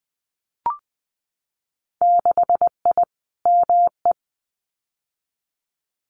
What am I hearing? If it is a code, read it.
Morse code, heard as a keyed tone, about 20 wpm, 700 Hz. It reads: 6I ME
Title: Winter Field Day Exchanges Encore 20wpm